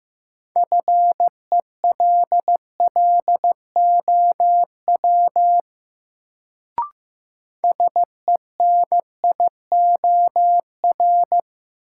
Morse code, heard as a keyed tone, about 15 wpm, 700 Hz. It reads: FELLOW SENIOR
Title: Sets of 5 Words Encore - New Word Drill - 15wpm